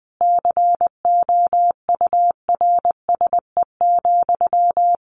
Morse code, heard as a keyed tone, about 19 wpm, 700 Hz. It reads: COURSE,
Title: Sentences from Top 1400 Words 20wpm